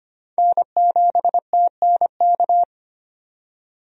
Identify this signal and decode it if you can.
Morse code, heard as a keyed tone, about 25 wpm, 700 Hz. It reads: N7TNK
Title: USA Call Signs - 5 Characters - 25wpm